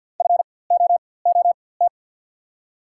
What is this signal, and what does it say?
Morse code, heard as a keyed tone, about 52 wpm, spaced as for 24 wpm, 690 Hz. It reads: FOOT
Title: 4x Character Spacing - Top 500 Words - 50wpm